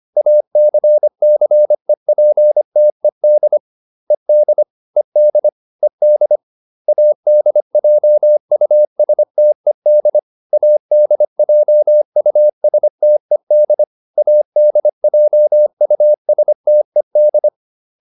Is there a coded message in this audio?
Yes: Morse code from 0.2 to 17.5 s, heard as a keyed tone, about 25 wpm, 600 Hz. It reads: ACCEPTED ED ED ED ADJUSTED ADJUSTED ADJUSTED